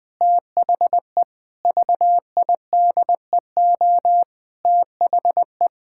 Morse code, heard as a keyed tone, about 20 wpm, 700 Hz. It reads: THE VIDEO THE
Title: Sentences with 7-Word Phrases - Spoken Beginning - 20wpm